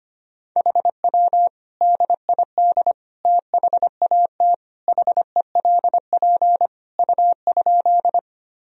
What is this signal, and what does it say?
Morse code, heard as a keyed tone, about 25 wpm, 700 Hz. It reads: HW DID THAT HELP U?
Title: Sentences from Top 200 Words and Abbreviations 25wpm